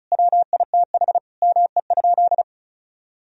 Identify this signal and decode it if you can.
Morse code, heard as a keyed tone, about 36 wpm, 700 Hz. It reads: WITH ME?
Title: Sentences from Top 1000 Words - Review Part 1 - 35wpm